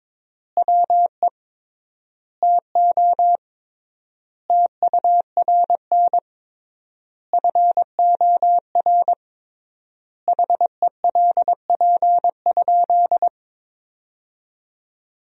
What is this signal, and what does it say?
Morse code, heard as a keyed tone, about 22 wpm, 700 Hz. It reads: WE TO TURN FOR HELP?
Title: 3x Word-Spacing - Sentences from Top 200 Words - 22wpm